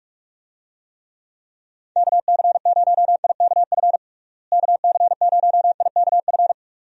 Morse code, heard as a keyed tone, about 41 wpm, 700 Hz. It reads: KC0IKF KC0IKF
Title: USA Call Signs - 6 Characters - Repeated 3x - 45wpm